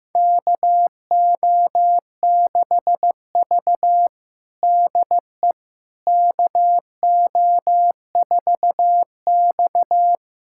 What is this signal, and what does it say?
Morse code, heard as a keyed tone, about 15 wpm, 700 Hz. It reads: KO6V DE KO4X
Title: USA Callsign DE Callsign - 4 Characters - 15wpm